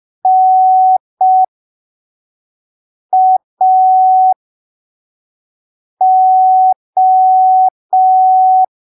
Morse code, heard as a keyed tone, about 5 wpm, spaced as for 3 wpm, 750 Hz. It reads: NAO